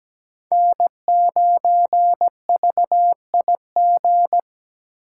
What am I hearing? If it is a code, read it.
Morse code, heard as a keyed tone, about 17 wpm, 700 Hz. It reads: N9VIG
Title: USA Call Signs - 4x Character Spacing - 17wpm